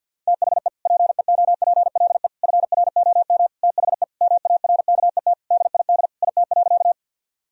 Morse code, heard as a keyed tone, about 50 wpm, 700 Hz. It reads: THE PEOPLE FROM THE MARKET DID IT.
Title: Sentences from Top 1000 Words - Review Part 3 - 50wpm